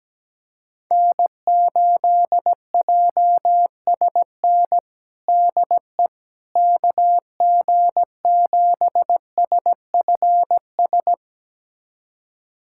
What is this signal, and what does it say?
Morse code, heard as a keyed tone, about 17 wpm, 700 Hz. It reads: N8JSN DE KG7SFS
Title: USA Callsign DE Callsign 17wpm